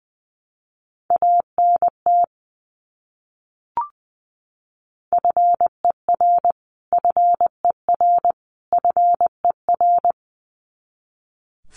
Morse code, heard as a keyed tone, about 20 wpm, 700 Hz. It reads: ANT FER FER FER
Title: Abbreviations Repeated 3x - 20wpm